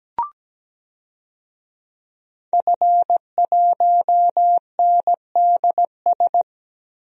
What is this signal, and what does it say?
Morse code, heard as a keyed tone, about 17 wpm, 700 Hz. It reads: F1NDS
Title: DXCC-100 Call Signs - 5 Characters - Repeated 3x - 17wpm